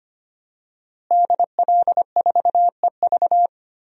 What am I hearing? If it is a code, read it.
Morse code, heard as a keyed tone, about 25 wpm, 700 Hz. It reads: DL4EV
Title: DXCC-100 Call Signs - 5 Characters - Repeated 3x - 25wpm